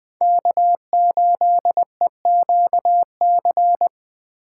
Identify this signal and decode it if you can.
Morse code, heard as a keyed tone, about 20 wpm, 700 Hz. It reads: K8EQC
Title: USA Call Signs - 5 Characters - 3x Character Spacing - 20wpm